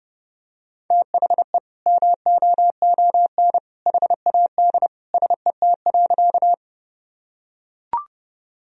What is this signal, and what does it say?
Morse code, heard as a keyed tone, about 32 wpm, 700 Hz. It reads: THE MOON HAD SET.
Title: Sentences from Top 1000 Words - Review Part 3 - 30wpm